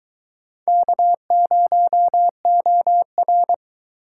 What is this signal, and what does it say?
Morse code, heard as a keyed tone, about 23 wpm, 700 Hz. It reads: K0OR